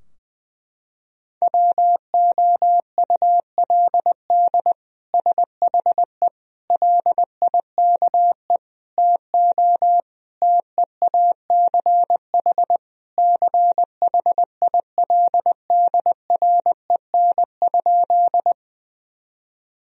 Morse code, heard as a keyed tone, about 21 wpm, 700 Hz. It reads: WOULD SHE LIKE TO TEACH CHILDREN?
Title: Sentences from Top 1000 Words - Review Part 2 - 20wpm